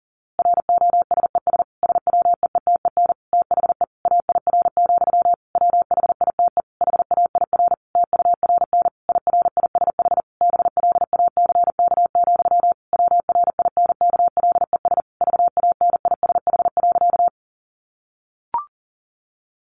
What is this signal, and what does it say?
Morse code, heard as a keyed tone, about 40 wpm, 700 Hz. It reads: ROSES SWEETEN THE AIR, WHITE HAIR TURN IRISH BLACK, WRINKLES VANISH.
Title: A Sound Of Thunder 40wpm